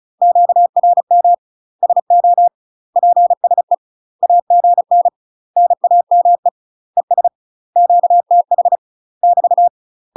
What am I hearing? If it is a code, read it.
Morse code, heard as a keyed tone, about 34 wpm, 700 Hz. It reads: QRM SO PSE AGN NAME ES QTH <BT>
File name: UK MASTER QSOs TRAINING / 35WPM / UK MASTER QSOs 0040